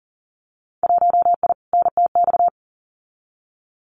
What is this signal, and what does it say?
Morse code, heard as a keyed tone, about 40 wpm, 700 Hz. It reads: 1I NTX
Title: Winter Field Day Exchanges 40wpm